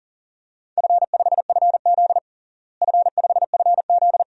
Morse code, heard as a keyed tone, about 40 wpm, 700 Hz. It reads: F5FZ F5FZ
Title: DXCC-100 Call Signs - 4 Characters - Repeated 2x - 40wpm